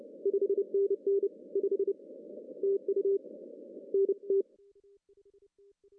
Morse code, heard as a keyed tone, about 28 wpm, 400 Hz. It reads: 5NN 5 TU NT
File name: CW / cqww-cw-k5tr